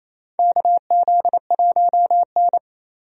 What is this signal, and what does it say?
Morse code, heard as a keyed tone, about 28 wpm, 700 Hz. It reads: KZ1N
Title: USA Call Signs - 4 Characters - Repeated 2x - 28wpm